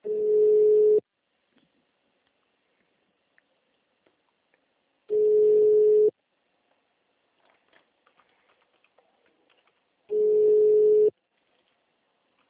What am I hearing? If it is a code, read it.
Ringback tone (Europe, 425 Hz).